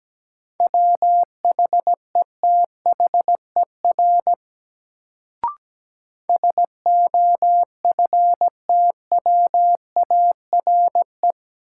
Morse code, heard as a keyed tone, about 17 wpm, 700 Hz. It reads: WHETHER SOFTWARE
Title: Sentences from Top 1000 Words - Review Part 3 - New Word Drill - 17wpm